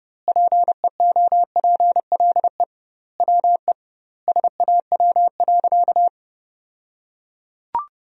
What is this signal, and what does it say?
Morse code, heard as a keyed tone, about 30 wpm, 700 Hz. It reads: PEOPLE WE SAW.
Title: 2x Word-Spacing - Sentences from Top 300 Words - 30wpm